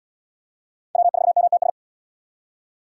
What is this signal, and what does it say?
Morse code, heard as a keyed tone, about 71 wpm, 700 Hz. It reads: W4MTS